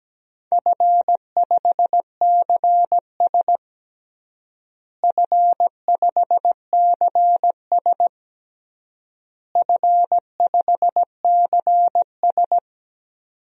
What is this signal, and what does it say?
Morse code, heard as a keyed tone, about 17 wpm, 700 Hz. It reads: F5CS F5CS F5CS
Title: DXCC-100 Call Signs - 4 Characters - Repeated 3x - 17wpm